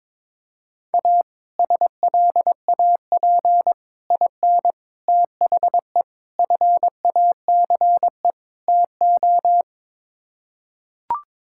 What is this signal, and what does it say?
Morse code, heard as a keyed tone, about 21 wpm, 700 Hz. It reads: A SLAP IN THE FACE TO
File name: practice / 22wpm / Sets of 6 Words - 22wpm